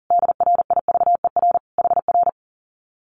Morse code, heard as a keyed tone, about 40 wpm, 700 Hz. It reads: DRIVER HR
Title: Common QSO Phrases Encore 40wpm